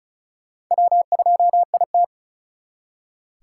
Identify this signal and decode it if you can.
Morse code, heard as a keyed tone, about 45 wpm, 700 Hz. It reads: W2IT